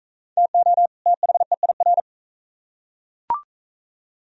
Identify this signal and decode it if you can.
Morse code, heard as a keyed tone, about 40 wpm, 700 Hz. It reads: TO THEIR